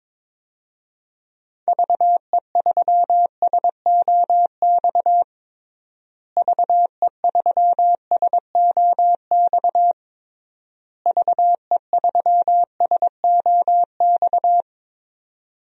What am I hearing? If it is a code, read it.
Morse code, heard as a keyed tone, about 22 wpm, 700 Hz. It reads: VE3SOX VE3SOX VE3SOX
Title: DXCC-100 Call Signs - 6 Characters - Repeated 3x - 22wpm